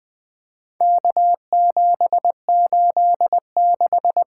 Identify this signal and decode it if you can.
Morse code, heard as a keyed tone, about 20 wpm, 700 Hz. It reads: K786
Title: Four Letter-Number Combinations - 20wpm